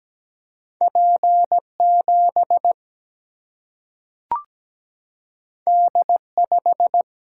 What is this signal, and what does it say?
Morse code, heard as a keyed tone, about 17 wpm, 700 Hz. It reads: P7 D5
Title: Single-Letter Single-Number 17wpm